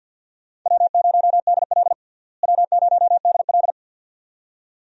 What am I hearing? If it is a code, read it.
Morse code, heard as a keyed tone, about 52 wpm, 700 Hz. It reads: W0DL W0DL
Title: USA Call Signs - 4 Characters - Repeated 2x - 50wpm